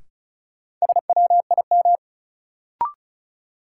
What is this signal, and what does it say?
Morse code, heard as a keyed tone, about 36 wpm, 700 Hz. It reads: SWIM
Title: Sentences from Top 1000 Words - Review Part 6 - New Word Drill - 35wpm